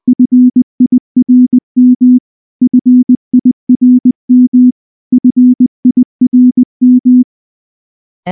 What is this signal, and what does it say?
Morse code, heard as a keyed tone, about 19 wpm, 250 Hz. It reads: FIRM FIRM FIRM
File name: morse / morse-training-20WPM